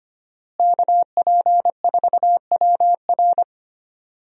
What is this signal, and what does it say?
Morse code, heard as a keyed tone, about 25 wpm, 700 Hz. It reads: KP4WR
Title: USA Call Signs - 5 Characters - Repeated 2x - 25wpm